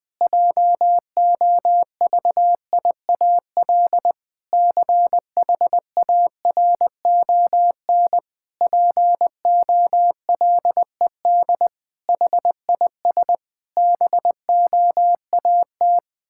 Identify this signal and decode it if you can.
Morse code, heard as a keyed tone, about 21 wpm, 700 Hz. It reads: JOVIAL CHARON POLED HIS BOAT
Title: Pangram Challenge v2 - 20wpm